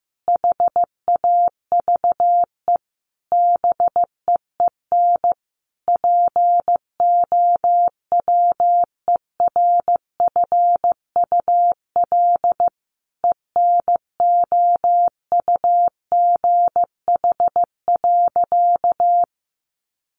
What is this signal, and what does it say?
Morse code, heard as a keyed tone, about 15 wpm, 700 Hz. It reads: HAVE BEEN POWERFUL ENOUGH.
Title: Sentences from Top 1300 Words 15wpm